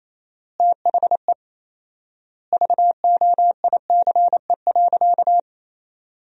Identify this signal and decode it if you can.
Morse code, heard as a keyed tone, about 28 wpm, 700 Hz. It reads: THE VOICE.
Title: 4x Word-Spacing - Sentences from Top 500 Words - 28wpm